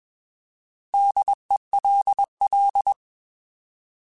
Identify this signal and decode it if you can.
Morse code, heard as a keyed tone, about 21 wpm, 800 Hz. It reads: DELL
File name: Alphabetical / Alphabetical mp3s / Ds @20